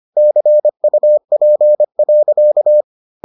Morse code, heard as a keyed tone, about 25 wpm, 600 Hz. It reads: CUP.